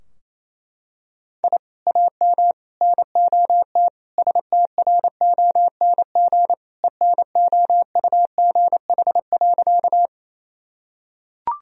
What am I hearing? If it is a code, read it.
Morse code, heard as a keyed tone, about 27 wpm, 700 Hz. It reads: I AM NOT STRONG ENOUGH.
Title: Sentences from Top 1000 Words - Review Part 3 - 28wpm